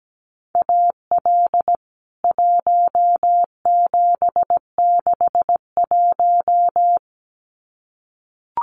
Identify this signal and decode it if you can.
Morse code, heard as a keyed tone, about 17 wpm, 700 Hz. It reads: AL 1761
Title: CWT Exchange 17wpm